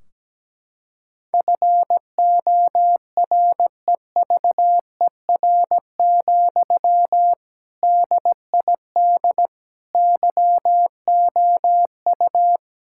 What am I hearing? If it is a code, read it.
Morse code, heard as a keyed tone, about 17 wpm, 700 Hz. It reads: FOREVER, DID YOU
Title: Sentences from Top 1600 Words 17wpm